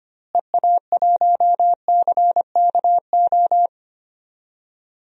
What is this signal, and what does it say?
Morse code, heard as a keyed tone, about 25 wpm, 700 Hz. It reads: EA1CKO
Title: DXCC-100 Call Signs - 6 Characters - Repeated 2x - 25wpm